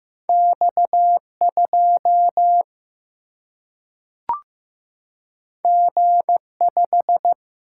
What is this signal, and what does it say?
Morse code, heard as a keyed tone, about 15 wpm, 700 Hz. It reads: X2 G5
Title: Single-Letter Single-Number 15wpm